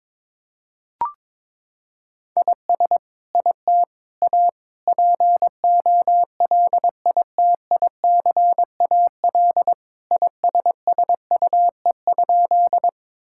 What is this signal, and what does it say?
Morse code, heard as a keyed tone, about 22 wpm, 700 Hz. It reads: IS IT A POLITICAL ISSUE?
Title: Sentences from Top 1000 Words - Review Part 6 - 22wpm